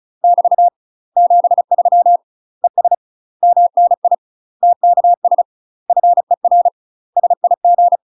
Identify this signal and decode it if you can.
Morse code, heard as a keyed tone, about 34 wpm, 700 Hz. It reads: <BT> 73 ES MNI TKS FER SIG
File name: UK MASTER QSOs TRAINING / 35WPM / UK MASTER QSOs 0008